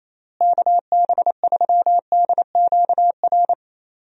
Morse code, heard as a keyed tone, about 28 wpm, 700 Hz. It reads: KB3DQR
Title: USA Call Signs - 6 Characters - Repeated 3x - 28wpm